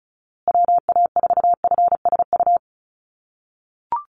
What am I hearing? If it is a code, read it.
Morse code, heard as a keyed tone, about 34 wpm, 700 Hz. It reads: WA4FSU
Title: USA Call Signs Repeated 2x 35wpm